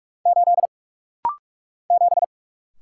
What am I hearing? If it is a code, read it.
Morse code, heard as a keyed tone, about 46 wpm, 700 Hz. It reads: 8 7
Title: Single Letter-Number Alt - Rapid-Fire 45wpm